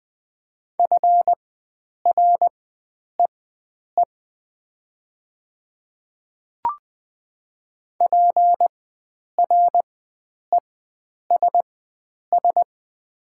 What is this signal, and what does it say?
Morse code, heard as a keyed tone, about 20 wpm, spaced as for 9 wpm, 700 Hz. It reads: FREE PRESS
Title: 4x Character Spacing - Top 500 Words - 20wpm